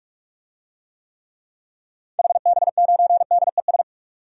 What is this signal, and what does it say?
Morse code, heard as a keyed tone, about 46 wpm, 700 Hz. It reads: HB9DES
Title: DXCC-100 Call Signs - 6 Characters - Repeated 2x - 45wpm